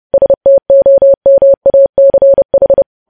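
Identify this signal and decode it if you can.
Morse code, heard as a keyed tone, about 30 wpm, 550 Hz. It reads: STOMACH
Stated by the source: News headlines sent in Morse code at 30 wpm.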